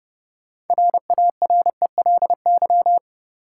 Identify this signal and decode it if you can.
Morse code, heard as a keyed tone, about 30 wpm, 700 Hz. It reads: RARELY
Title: Sentences from Top 1400 Words - New Word Drill Alt - 30wpm